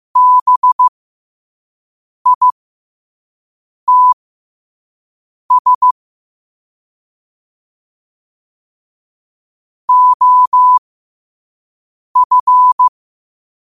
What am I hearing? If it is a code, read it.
Morse code, heard as a keyed tone, about 15 wpm, spaced as for 5 wpm, 1000 Hz. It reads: BITS OF
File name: qotdfiles / 5wpm